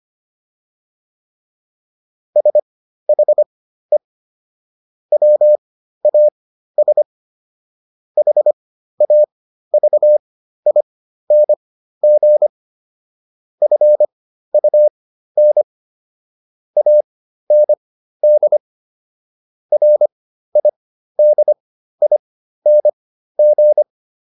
Morse code, heard as a keyed tone, about 25 wpm, spaced as for 13 wpm, 600 Hz. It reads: SHE WAS HAVING FUN AND RIDING